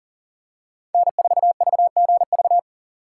Morse code, heard as a keyed tone, about 40 wpm, 700 Hz. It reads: N4VGV
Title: USA Call Signs - 5 Characters - 6x Character Spacing - 40wpm